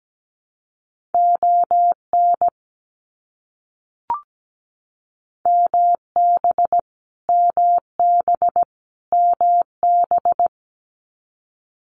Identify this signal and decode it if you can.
Morse code, heard as a keyed tone, about 17 wpm, 700 Hz. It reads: ON MB MB MB
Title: Canadian Provinces and Territories - Repeated 3x - 17wpm